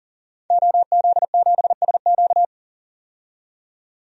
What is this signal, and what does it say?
Morse code, heard as a keyed tone, about 40 wpm, 700 Hz. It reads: OZ7SQ
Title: DXCC-100 Call Signs - 5 Characters - 40wpm